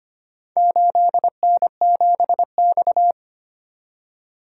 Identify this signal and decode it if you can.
Morse code, heard as a keyed tone, about 25 wpm, 700 Hz. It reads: 8N7X
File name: practice / 25wpm / Four Letter-Number Combinations - 25wpm